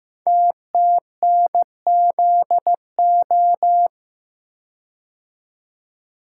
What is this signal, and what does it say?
Morse code, heard as a keyed tone, about 15 wpm, 700 Hz. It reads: TTNZO
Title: Code Groups - 5 Letters - 15wpm